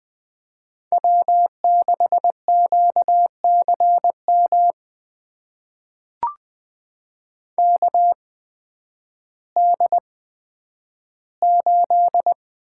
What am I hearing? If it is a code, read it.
Morse code, heard as a keyed tone, about 20 wpm, spaced as for 11 wpm, 700 Hz. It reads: W6QCM KD8
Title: USA Call Signs - 5 Characters - 8x Character Spacing - 20wpm